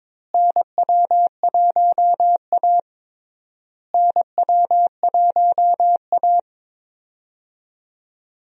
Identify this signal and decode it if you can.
Morse code, heard as a keyed tone, about 22 wpm, 700 Hz. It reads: NW1A NW1A
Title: USA Call Signs - 4 Characters - Repeated 3x - 22wpm